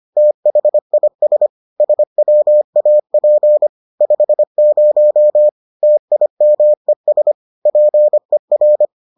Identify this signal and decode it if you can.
Morse code, heard as a keyed tone, about 25 wpm, 600 Hz. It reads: THIS SWAP 50 TIMES PER